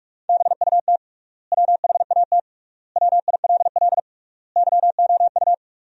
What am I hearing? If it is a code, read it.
Morse code, heard as a keyed tone, about 46 wpm, 700 Hz. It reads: BUT WHAT WILL YOU
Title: 3x Word-Spacing - Sentences from Top 100 Words - 45wpm